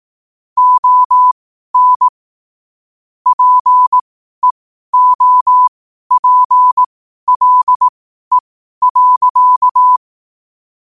Morse code, heard as a keyed tone, about 18 wpm, spaced as for 12 wpm, 1000 Hz. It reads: ON PEOPLE.
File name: qotdfiles / 12wpm